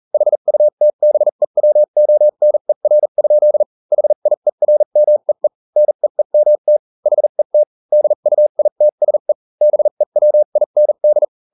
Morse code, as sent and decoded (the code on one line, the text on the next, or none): .... ..- - -... .-- --- -. .-. ..--.. / .... .. .-. -- . / -. . -- - / .... - / -.. ..- .. - ... / -... .-- .. -. -..
HUTBEWONER? HIERMEE NEEMT HET DUITSE BEWIND